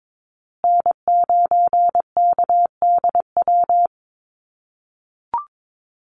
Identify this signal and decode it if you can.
Morse code, heard as a keyed tone, about 22 wpm, 700 Hz. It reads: N9KDW
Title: USA Call Signs Repeated 3x 22wpm